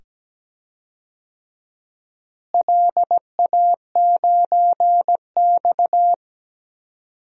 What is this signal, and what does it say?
Morse code, heard as a keyed tone, about 17 wpm, 700 Hz. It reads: LA9X
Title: DXCC-100 Call Signs - 4 Characters - Repeated 2x - 17wpm